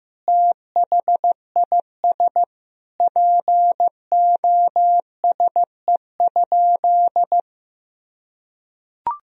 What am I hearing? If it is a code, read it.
Morse code, heard as a keyed tone, about 15 wpm, 700 Hz. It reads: THIS POSE?
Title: Sentences from Top 1000 Words - Review Part 2 - 15wpm